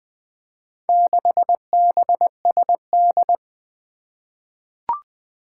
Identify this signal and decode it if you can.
Morse code, heard as a keyed tone, about 21 wpm, 700 Hz. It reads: 6BSD
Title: Four Challenging Letter-Number Combinations 20wpm